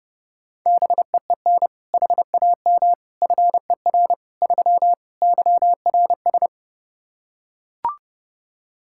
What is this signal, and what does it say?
Morse code, heard as a keyed tone, about 30 wpm, 700 Hz. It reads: BEEN HAM FER 3 YRS
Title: Practice QSO - Exchange 3 - More Info - 30wpm